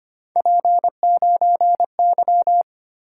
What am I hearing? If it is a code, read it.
Morse code, heard as a keyed tone, about 29 wpm, 700 Hz. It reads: P9Y
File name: Speed-Racing - Three Letter-Number Combinations - 25wpm